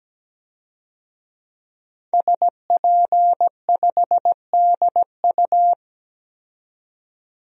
Morse code, heard as a keyed tone, about 17 wpm, 700 Hz. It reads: SP5DU
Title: DXCC-100 Call Signs - 5 Characters - 17wpm